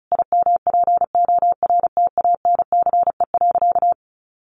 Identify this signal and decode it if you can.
Morse code, heard as a keyed tone, about 34 wpm, 700 Hz. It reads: IMPORTANCE.
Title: Sentences from Top 1200 Words 35wpm